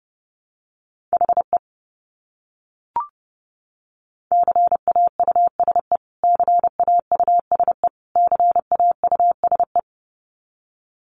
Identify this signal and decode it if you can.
Morse code, heard as a keyed tone, about 30 wpm, 700 Hz. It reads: HE CAUSE CAUSE CAUSE
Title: Top 200 Words Repeated 30wpm